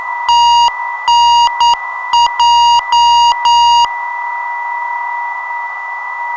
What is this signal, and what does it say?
Morse code, heard as a keyed tone, about 9 wpm, 2850 Hz, with noise. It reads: TNJ
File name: ndb zvok / GEN-318kHz 400-24-8Hz filter